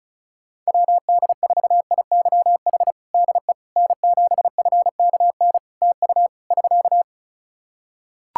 Morse code, heard as a keyed tone, about 36 wpm, 700 Hz. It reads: WD4IYH DE N7FKN TU <SK>
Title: Practice QSO - Exchange 4 - Ending - 35wpm